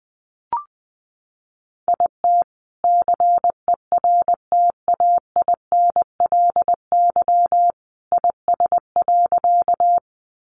Morse code, heard as a keyed tone, about 20 wpm, 700 Hz. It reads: IT CERTAINLY IS.
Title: Sentences from Top 900 Words 20wpm